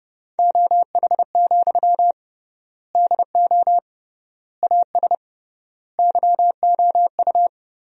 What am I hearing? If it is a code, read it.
Morse code, heard as a keyed tone, about 30 wpm, 700 Hz. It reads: OH, DO AS YOU
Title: 3x Word-Spacing - Sentences from Top 500 Words - 30wpm